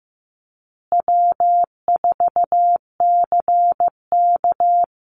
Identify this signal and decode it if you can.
Morse code, heard as a keyed tone, about 15 wpm, 700 Hz. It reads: W4CK
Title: Easy Call Signs Repeated 2x - 15wpm